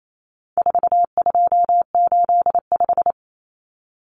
Morse code, heard as a keyed tone, about 28 wpm, 700 Hz. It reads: 4285
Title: Common Cut Numbers - 4-Digit 28wpm